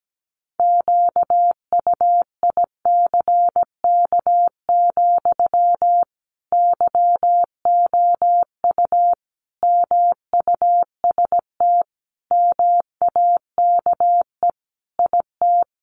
Morse code, heard as a keyed tone, about 17 wpm, 700 Hz. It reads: QUICK, YOU MUST MAKE IT
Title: Sentences from Top 500 Words 17wpm